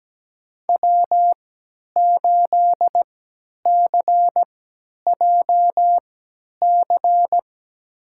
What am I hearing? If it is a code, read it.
Morse code, heard as a keyed tone, about 17 wpm, spaced as for 10 wpm, 700 Hz. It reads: W8CJC
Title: USA Call Signs - 5 Characters - 3x Character Spacing - 17wpm